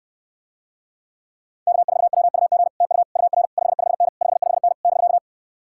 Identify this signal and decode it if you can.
Morse code, heard as a keyed tone, about 69 wpm, 700 Hz. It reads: K4WUG TU UR 55N 55N <BK>